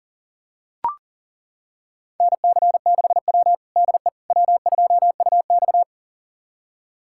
Morse code, heard as a keyed tone, about 40 wpm, 700 Hz. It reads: NC6W DE W2UX
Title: USA Callsign DE Callsign - 4 Characters - 40wpm